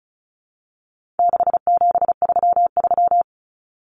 Morse code, heard as a keyed tone, about 34 wpm, 700 Hz. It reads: 6733
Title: Numbers - 4-Digit 35wpm